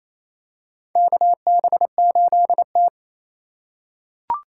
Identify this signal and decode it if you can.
Morse code, heard as a keyed tone, about 28 wpm, 700 Hz. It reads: KB8T
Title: USA Call Signs - 4 Characters - 3x Character Spacing - 28wpm